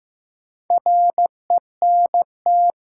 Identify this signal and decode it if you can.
Morse code, heard as a keyed tone, about 15 wpm, 700 Hz. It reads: RENT
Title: Sentences from Top 1000 Words - Review Part 5 - New Word Drill - 15wpm